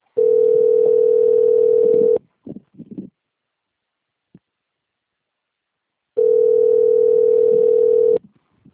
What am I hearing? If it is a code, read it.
Ringback tone (North America), 440 + 480 Hz.